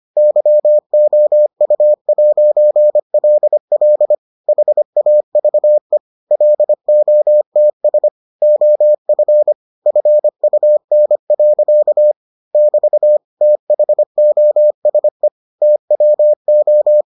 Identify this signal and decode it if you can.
Morse code, heard as a keyed tone, about 25 wpm, 600 Hz. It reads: YOU'LL HAVE LOTS OF FUN. <BT> THOSE TWO